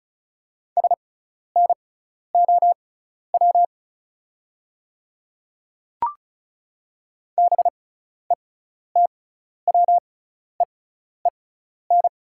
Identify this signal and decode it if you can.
Morse code, heard as a keyed tone, about 34 wpm, spaced as for 12 wpm, 690 Hz. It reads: SNOW BETWEEN
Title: 6x Character Spacing - Top 500 Words - 35wpm